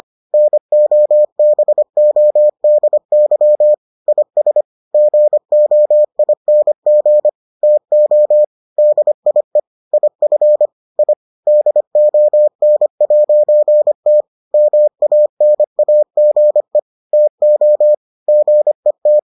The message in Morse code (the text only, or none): NOBODY IS GOING TO DIE IF I DON'T MANAGE TO GET